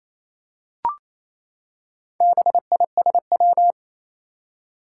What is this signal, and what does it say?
Morse code, heard as a keyed tone, about 27 wpm, 700 Hz. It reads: BISW
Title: Four Challenging Letter-Number Combinations 28wpm